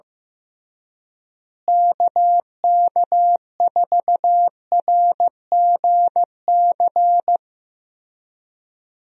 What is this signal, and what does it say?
Morse code, heard as a keyed tone, about 15 wpm, 700 Hz. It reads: KK4RGC
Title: USA Call Signs - 6 Characters - Repeated 3x - 15wpm